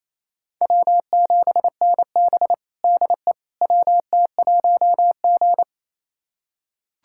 Morse code, heard as a keyed tone, about 28 wpm, 700 Hz. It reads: W7NB DE WT1G
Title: USA Callsign DE Callsign - 4 Characters - 28wpm